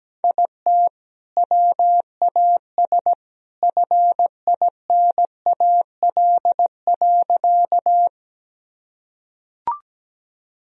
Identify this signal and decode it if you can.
Morse code, heard as a keyed tone, about 17 wpm, 700 Hz. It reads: IT WAS FINAL.
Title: Sentences from Top 1000 Words - Review Part 2 - 17wpm